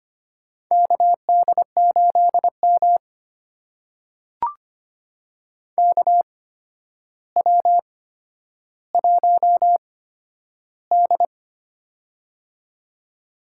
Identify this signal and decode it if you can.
Morse code, heard as a keyed tone, about 25 wpm, spaced as for 11 wpm, 700 Hz. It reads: KD8M KW1D
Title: USA Call Signs - 4 Characters - 8x Character Spacing - 25wpm